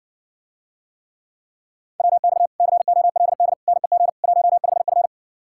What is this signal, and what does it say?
Morse code, heard as a keyed tone, about 70 wpm, 700 Hz. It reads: WX COLD NEG 15F